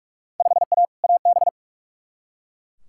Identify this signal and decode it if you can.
Morse code, heard as a keyed tone, about 42 wpm, 700 Hz. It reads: 5A AB
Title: Field Day Exchanges Encore - 45wpm